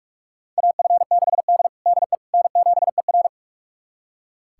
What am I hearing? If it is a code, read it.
Morse code, heard as a keyed tone, about 41 wpm, 700 Hz. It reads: AF6D DE N7ER
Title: USA Callsign DE Callsign - 4 Characters - 45wpm